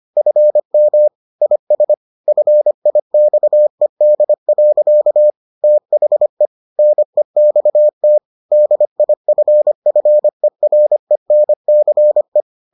Morse code, heard as a keyed tone, about 25 wpm, 600 Hz. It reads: FM IS FIXED. THE NEXT DIFFERENCE